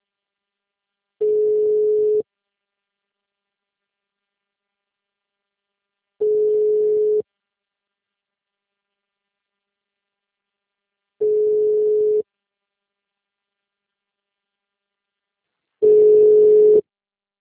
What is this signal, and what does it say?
Ringback tone (Europe, 425 Hz).
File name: out- -2019- - - .26503